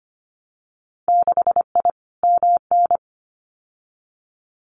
Morse code, heard as a keyed tone, about 25 wpm, 700 Hz. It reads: 6I MN
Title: Winter Field Day Exchanges Encore 25wpm